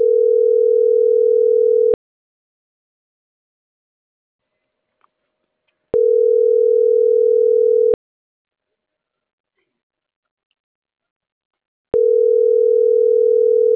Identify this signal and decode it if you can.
Ringback tone (North America), 440 + 480 Hz.